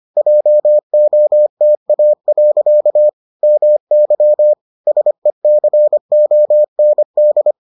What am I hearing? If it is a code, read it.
Morse code, heard as a keyed tone, about 25 wpm, 600 Hz. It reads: JOTA. MY SECOND